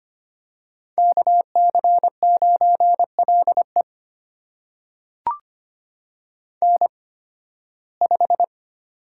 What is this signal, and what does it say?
Morse code, heard as a keyed tone, about 25 wpm, spaced as for 16 wpm, 700 Hz. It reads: KC9LE N5
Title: USA Call Signs - 5 Characters - 8x Character Spacing - 25wpm